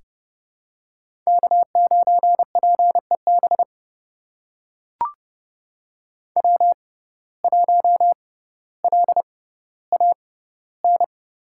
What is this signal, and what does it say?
Morse code, heard as a keyed tone, about 30 wpm, spaced as for 15 wpm, 700 Hz. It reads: K9PEB W1LAN
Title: USA Call Signs - 5 Characters - 6x Character Spacing - 30wpm